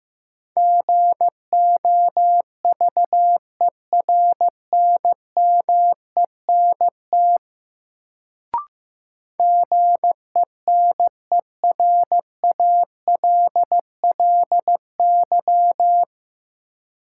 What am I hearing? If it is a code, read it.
Morse code, heard as a keyed tone, about 15 wpm, 700 Hz. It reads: GOVERNMENT GENERALLY
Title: Sentences from Top 600 Words - New Word Drill - 15wpm